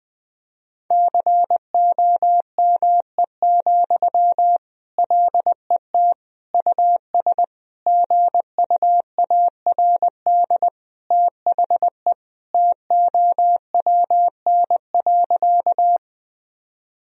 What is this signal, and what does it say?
Morse code, heard as a keyed tone, about 20 wpm, 700 Hz. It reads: COME, LET US GUARD THE TOWN.
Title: Sentences from Top 1000 Words - Review Part 4 - 20wpm